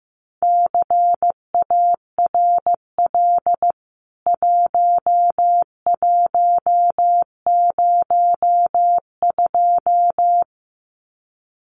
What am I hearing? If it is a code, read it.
Morse code, heard as a keyed tone, about 15 wpm, 700 Hz. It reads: CARL 1102
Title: CWT Exchange 15wpm